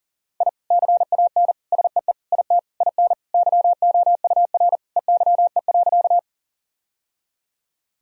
Morse code, heard as a keyed tone, about 40 wpm, 700 Hz. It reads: I CAN SEE IT IN YOUR EYE.
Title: Sentences from Top 1000 Words - Review Part 1 - 40wpm